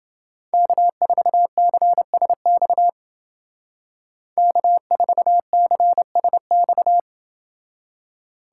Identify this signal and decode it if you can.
Morse code, heard as a keyed tone, about 28 wpm, 700 Hz. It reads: K4CSX K4CSX